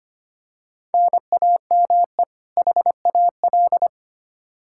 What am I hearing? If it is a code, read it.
Morse code, heard as a keyed tone, about 25 wpm, 700 Hz. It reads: NAME HAL
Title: Practice QSO - Exchange 1 - Beginning - 25wpm